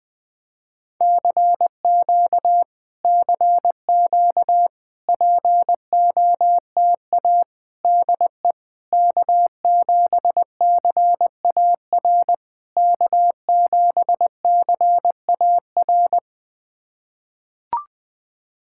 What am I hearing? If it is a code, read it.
Morse code, heard as a keyed tone, about 20 wpm, 690 Hz. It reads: CQ CQ POTA DE K7CAR K7CAR
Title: POTA QSO Practice - 20wpm